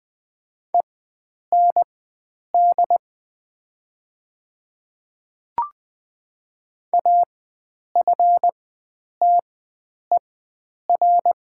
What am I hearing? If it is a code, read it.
Morse code, heard as a keyed tone, about 20 wpm, spaced as for 9 wpm, 700 Hz. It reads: END AFTER
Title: 4x Character Spacing - Top 500 Words - 20wpm